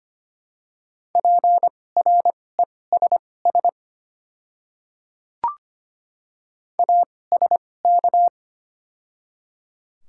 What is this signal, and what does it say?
Morse code, heard as a keyed tone, about 25 wpm, spaced as for 18 wpm, 700 Hz. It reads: PRESS ASK
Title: 2x Character Spacing - Top 300 Words - 25wpm